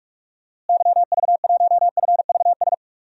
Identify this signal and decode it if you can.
Morse code, heard as a keyed tone, about 43 wpm, 700 Hz. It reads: YV1FVS
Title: DXCC-100 Call Signs - 6 Characters - Repeated 2x - 45wpm